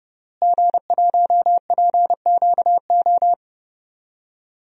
Morse code, heard as a keyed tone, about 30 wpm, 700 Hz. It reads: G1PQO
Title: DXCC-100 Call Signs - 5 Characters - Repeated 3x - 30wpm